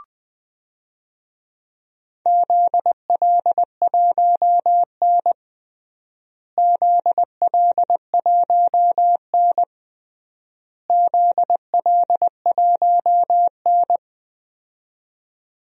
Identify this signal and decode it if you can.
Morse code, heard as a keyed tone, about 19 wpm, 700 Hz. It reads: ZL1N ZL1N ZL1N
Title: DXCC-100 Call Signs - 4 Characters - Repeated 3x - 20wpm